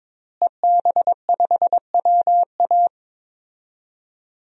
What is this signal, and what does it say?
Morse code, heard as a keyed tone, about 21 wpm, 700 Hz. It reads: EB5WA
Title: DXCC-100 Call Signs - 5 Characters - Repeated 3x - 22wpm